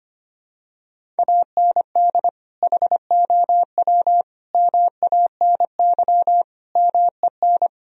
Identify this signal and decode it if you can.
Morse code, heard as a keyed tone, about 25 wpm, 700 Hz. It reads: AND HOW MANY MEN
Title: Sentences from Top 1000 Words - Review Part 1 - 25wpm